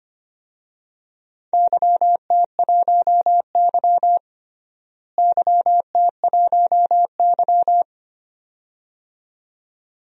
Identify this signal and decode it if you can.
Morse code, heard as a keyed tone, about 25 wpm, 700 Hz. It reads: YT1Y YT1Y
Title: DXCC-100 Call Signs - 4 Characters - Repeated 2x - 25wpm